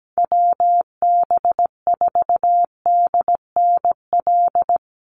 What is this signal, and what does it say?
Morse code, heard as a keyed tone, about 17 wpm, 700 Hz. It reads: WB4DNL
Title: Call Signs Repeated 2x - 17wpm